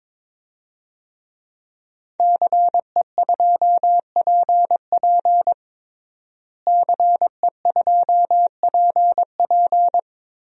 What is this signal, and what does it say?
Morse code, heard as a keyed tone, about 22 wpm, 700 Hz. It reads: CE2PP CE2PP
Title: DXCC-100 Call Signs - 5 Characters - Repeated 3x - 22wpm